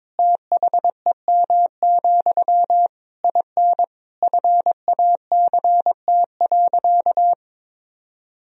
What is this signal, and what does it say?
Morse code, heard as a keyed tone, about 22 wpm, 700 Hz. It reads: THEM, IN FACT.
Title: Sentences from Top 1000 Words - Review Part 3 - 22wpm